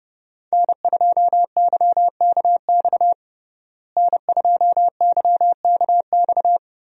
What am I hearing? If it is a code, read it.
Morse code, heard as a keyed tone, about 30 wpm, 700 Hz. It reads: N2YKX N2YKX
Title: USA Call Signs - 5 Characters - Repeated 2x - 30wpm